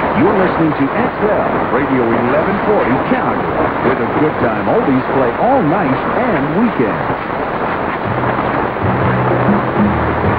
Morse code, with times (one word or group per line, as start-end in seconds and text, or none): none